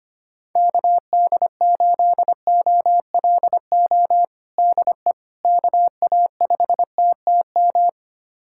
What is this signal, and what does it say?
Morse code, heard as a keyed tone, about 25 wpm, 700 Hz. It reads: KD8OLO DE KA5TTM
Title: USA Callsign DE Callsign 25wpm